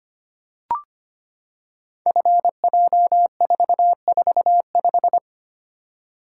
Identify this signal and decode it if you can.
Morse code, heard as a keyed tone, about 25 wpm, 700 Hz. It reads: FJ445
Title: Code Groups - 5 Letter:Numbers - 25wpm